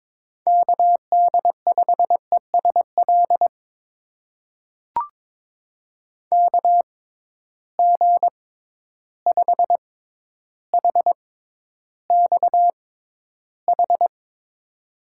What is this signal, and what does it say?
Morse code, heard as a keyed tone, about 22 wpm, spaced as for 11 wpm, 700 Hz. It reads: KD5ESL KG5HXH
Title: USA Call Signs - 6x Character Spacing - 22wpm